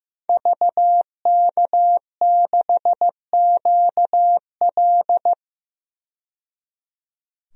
Morse code, heard as a keyed tone, about 15 wpm, 700 Hz. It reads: VK6QL
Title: DXCC-100 Call Signs - 5 Characters - 15wpm